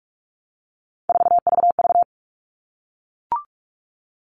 Morse code, heard as a keyed tone, about 44 wpm, 700 Hz. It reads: 4VV
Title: All Cut Numbers - 3-Digit 45wpm